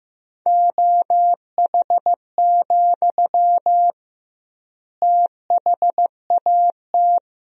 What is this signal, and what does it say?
Morse code, heard as a keyed tone, about 15 wpm, 700 Hz. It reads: OH, THAT
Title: 2x Word-Spacing - Sentences from Top 500 Words - 15wpm